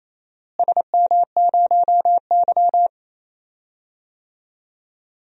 Morse code, heard as a keyed tone, about 28 wpm, 700 Hz. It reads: SM0Y
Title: DXCC-100 Call Signs - 4 Characters - Repeated 2x - 28wpm